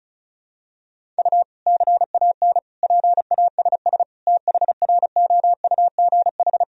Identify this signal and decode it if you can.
Morse code, heard as a keyed tone, about 36 wpm, 700 Hz. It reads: U CAN PASS THROUGH
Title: Sentences from Top 400 Words and Abbreviations 35wpm